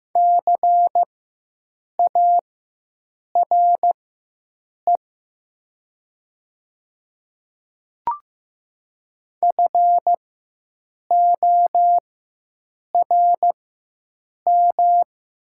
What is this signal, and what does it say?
Morse code, heard as a keyed tone, about 15 wpm, spaced as for 7 wpm, 700 Hz. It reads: CARE FORM
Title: 4x Character Spacing - Top 500 Words - 15wpm